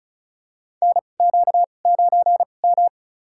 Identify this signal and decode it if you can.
Morse code, heard as a keyed tone, about 34 wpm, spaced as for 25 wpm, 700 Hz. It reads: NQ9M
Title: USA Call Signs - 2x Character Spacing - 35wpm